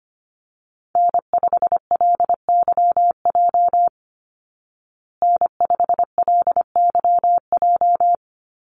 Morse code, heard as a keyed tone, about 25 wpm, 700 Hz. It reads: N5LYJ N5LYJ
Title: Call Signs Repeated 3x - 25wpm